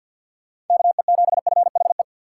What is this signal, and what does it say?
Morse code, heard as a keyed tone, about 49 wpm, 700 Hz. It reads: KE7FHE
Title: USA Call Signs - 6 Characters - 3x Character Spacing - 50wpm